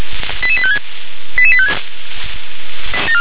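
Five-tone selective call (ZVEI), heard as quick runs of five tones, tones of about 70 ms. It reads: 88856 88856
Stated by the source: ZVEI five-tone call 88775 received by SDR.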